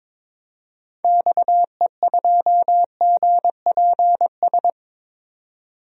Morse code, heard as a keyed tone, about 22 wpm, 700 Hz. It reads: XE2GPS
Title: DXCC-100 Call Signs - 6 Characters - 22wpm